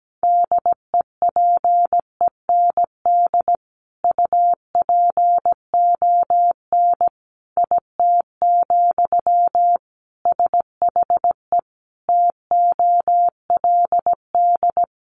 Morse code, heard as a keyed tone, about 17 wpm, 700 Hz. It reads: DEPEND UPON IT, SHE TOLD
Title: Sentences from Top 1300 Words 17wpm